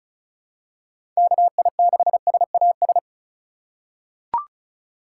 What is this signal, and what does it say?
Morse code, heard as a keyed tone, about 34 wpm, 700 Hz. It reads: KI6SAS
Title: USA Call Signs - 3x Character Spacing - 35wpm